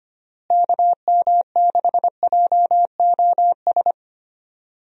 Morse code, heard as a keyed tone, about 25 wpm, 700 Hz. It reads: KM6JOS
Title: USA Call Signs - 6 Characters - Repeated 3x - 25wpm